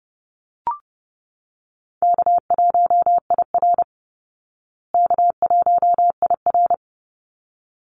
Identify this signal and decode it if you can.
Morse code, heard as a keyed tone, about 30 wpm, 700 Hz. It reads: K1IR K1IR
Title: Easy Call Signs Repeated 2x - 30wpm